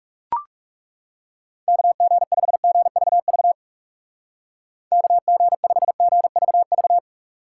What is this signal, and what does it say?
Morse code, heard as a keyed tone, about 37 wpm, 700 Hz. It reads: KG5GVV KG5GVV
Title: Speed Racing - USA 2x3 Call Signs 30wpm